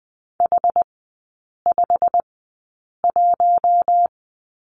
Morse code, heard as a keyed tone, about 20 wpm, spaced as for 8 wpm, 700 Hz. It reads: H51
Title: 087 Lesson 15e - Number 5 - Three Characters 20wpm